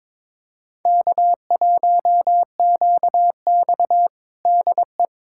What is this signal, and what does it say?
Morse code, heard as a keyed tone, about 21 wpm, 700 Hz. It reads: K1QX DE
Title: Practice QSO - Exchange 4 - Ending - 22wpm